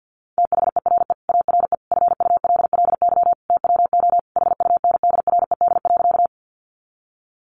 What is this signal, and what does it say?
Morse code, heard as a keyed tone, about 50 wpm, 700 Hz. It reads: THERE ARE FULLY TWO HUNDRED.
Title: Sentences from Top 1300 Words 50wpm